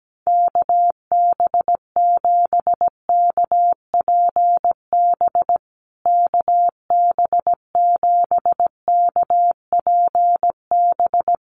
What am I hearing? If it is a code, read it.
Morse code, heard as a keyed tone, about 17 wpm, 700 Hz. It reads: KB7KPB KB7KPB
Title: USA Call Signs Repeated 3x 17wpm